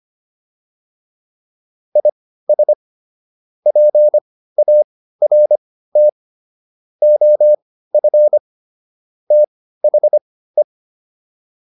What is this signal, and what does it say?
Morse code, heard as a keyed tone, about 25 wpm, spaced as for 15 wpm, 600 Hz. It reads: IS PART OF THE